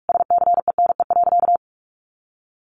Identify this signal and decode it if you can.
Morse code, heard as a keyed tone, about 44 wpm, 700 Hz. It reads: SCENE.
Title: Sentences from Top 1100 Words 45wpm